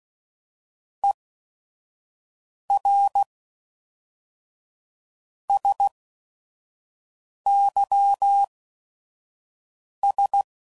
Morse code, heard as a keyed tone, about 16 wpm, spaced as for 4 wpm, 800 Hz. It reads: ERSYS
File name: CW Code Groups @5WPM / Technical Subjects / SwPwrSup